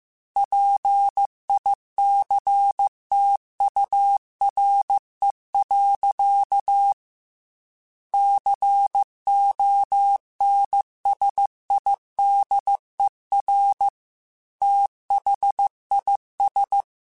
Morse code, heard as a keyed tone, about 15 wpm, 800 Hz. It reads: PICTURE. CONSIDER THIS
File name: Technical Subjects / Technical Subjects-mp3s / AND Gate @13